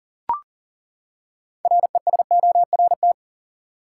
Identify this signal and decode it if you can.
Morse code, heard as a keyed tone, about 41 wpm, 700 Hz. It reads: RESORT
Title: Six Letter Words - 40wpm